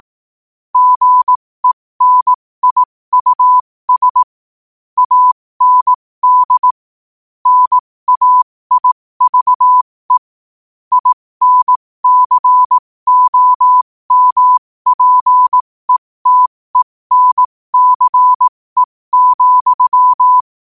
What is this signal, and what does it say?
Morse code, heard as a keyed tone, about 18 wpm, spaced as for 15 wpm, 1000 Hz. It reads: GENIUS AND NAIVE INCOMPETENCE,
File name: qotdfiles / 15wpm